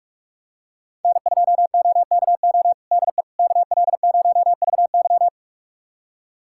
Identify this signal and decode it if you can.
Morse code, heard as a keyed tone, about 46 wpm, 690 Hz. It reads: N2OKO DE KL0VY
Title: USA Callsign DE Callsign 45wpm